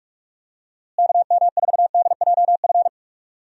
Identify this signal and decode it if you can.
Morse code, heard as a keyed tone, about 41 wpm, 700 Hz. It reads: KM4DJF
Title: USA Call Signs - 6 Characters - Repeated 2x - 45wpm